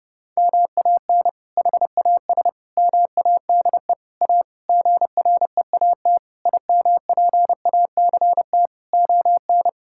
Morse code, heard as a keyed tone, about 30 wpm, 700 Hz. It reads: MAN HAS MADE A GREAT IMPACT ON
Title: Sentences from Top 1000 Words - Review Part 6 - 30wpm